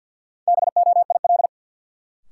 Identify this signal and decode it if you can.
Morse code, heard as a keyed tone, about 52 wpm, 700 Hz. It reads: BOIL
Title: Sets of 5 Words Encore - New Word Drill - 50wpm